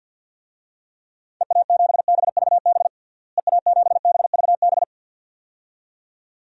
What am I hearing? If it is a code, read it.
Morse code, heard as a keyed tone, about 47 wpm, 700 Hz. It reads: EA7BVB EA7BVB
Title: DXCC-100 Call Signs - 6 Characters - Repeated 2x - 50wpm